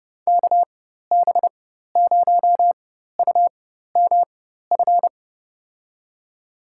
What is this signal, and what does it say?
Morse code, heard as a keyed tone, about 30 wpm, spaced as for 14 wpm, 700 Hz. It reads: KB0UMF
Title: USA Call Signs - 6 Characters - 4x Character Spacing - 30wpm